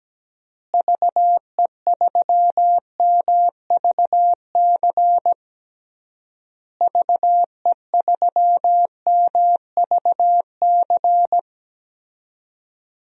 Morse code, heard as a keyed tone, about 17 wpm, 700 Hz. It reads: VE3MVC VE3MVC
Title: DXCC-100 Call Signs - 6 Characters - Repeated 3x - 17wpm